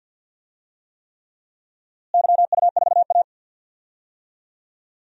Morse code, heard as a keyed tone, about 50 wpm, 700 Hz. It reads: YU4A
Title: DXCC-100 Call Signs - 4 Characters - Repeated 3x - 50wpm